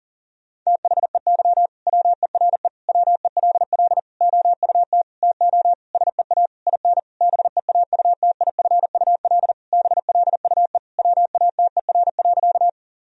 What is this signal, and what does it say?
Morse code, heard as a keyed tone, about 40 wpm, 700 Hz. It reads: THEY WERE WELL OUT TO SEA IN BEAUTIFUL BLUE WATER.
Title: Sentences from Top 1000 Words - Review Part 3 - 40wpm